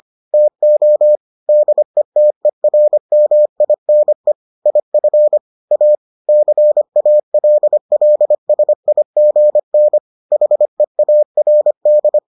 Morse code, heard as a keyed tone, about 25 wpm, 600 Hz. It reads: TO DETERMINE IF A CALLSIGN HEARD